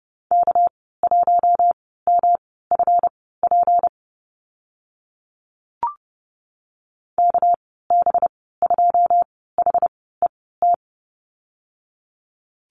Morse code, heard as a keyed tone, about 30 wpm, spaced as for 17 wpm, 700 Hz. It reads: K1MFP KB2HET
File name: Lesson 37g - Letter x - Call Signs 30wpm